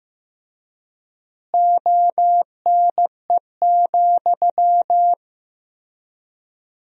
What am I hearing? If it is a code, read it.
Morse code, heard as a keyed tone, about 15 wpm, 700 Hz. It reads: ONE,
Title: 3x Word-Spacing - Sentences from Top 200 Words - 15wpm